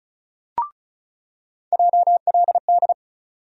Morse code, heard as a keyed tone, about 36 wpm, 700 Hz. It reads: JLD
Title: USA Call Sign Postfixes - 3 Letters 35wpm